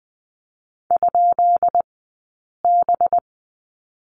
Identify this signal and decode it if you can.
Morse code, heard as a keyed tone, about 20 wpm, spaced as for 8 wpm, 700 Hz. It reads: ?B
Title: 217 Lesson 35d - Number 0 - Two Characters 20wpm